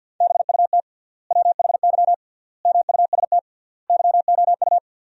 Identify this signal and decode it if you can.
Morse code, heard as a keyed tone, about 50 wpm, 700 Hz. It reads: BUT WHY MUST YOU
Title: 3x Word-Spacing - Sentences from Top 400 Words - 50wpm